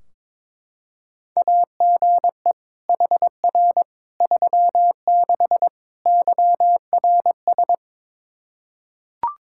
Morse code, heard as a keyed tone, about 21 wpm, 700 Hz. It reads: AGE HR 36 YRS
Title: Practice QSO - Exchange 3 - More Info - 22wpm